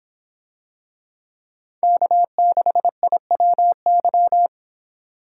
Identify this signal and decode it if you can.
Morse code, heard as a keyed tone, about 26 wpm, 700 Hz. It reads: K6IWY